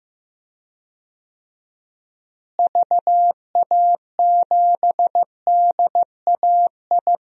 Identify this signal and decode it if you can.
Morse code, heard as a keyed tone, about 15 wpm, 700 Hz. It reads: VA7DAI
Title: DXCC-100 Call Signs - 6 Characters - Repeated 3x - 15wpm